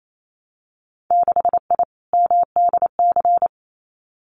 Morse code, heard as a keyed tone, about 28 wpm, 700 Hz. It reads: 6I MDC
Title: Winter Field Day Exchanges Encore 28wpm